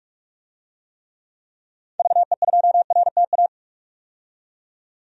Morse code, heard as a keyed tone, about 44 wpm, 690 Hz. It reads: VE2RTA
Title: DXCC-100 Call Signs - 6 Characters - 45wpm